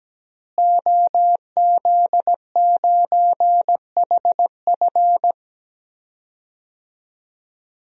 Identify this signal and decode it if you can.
Morse code, heard as a keyed tone, about 17 wpm, 700 Hz. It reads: OZ9HF
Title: DXCC-100 Call Signs - 5 Characters - Repeated 3x - 17wpm